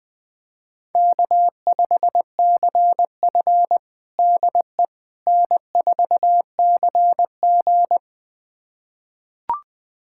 Morse code, heard as a keyed tone, about 20 wpm, 700 Hz. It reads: K5CF DE N4CG
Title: USA Callsign DE Callsign - 4 Characters - 20wpm